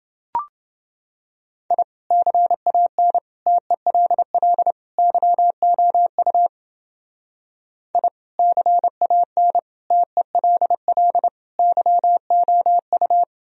Morse code, heard as a keyed tone, about 29 wpm, 700 Hz. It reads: I CAN TELL YOU I CAN TELL YOU